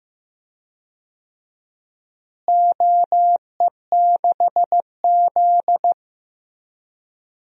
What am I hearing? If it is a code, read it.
Morse code, heard as a keyed tone, about 15 wpm, 700 Hz. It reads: OE6Z
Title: DXCC-100 Call Signs - 4 Characters - Repeated 2x - 15wpm